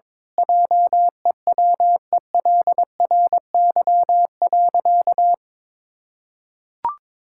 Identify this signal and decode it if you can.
Morse code, heard as a keyed tone, about 22 wpm, 700 Hz. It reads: JEWELRY.
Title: Pangram Challenge v2 - 22wpm